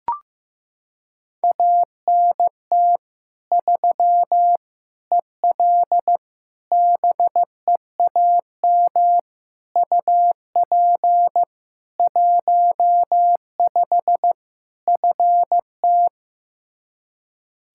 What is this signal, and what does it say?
Morse code, heard as a keyed tone, about 15 wpm, 700 Hz. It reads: ANT 3 EL BEAM UP 15 FT
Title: Practice QSO - Exchange 2 - Info - 15wpm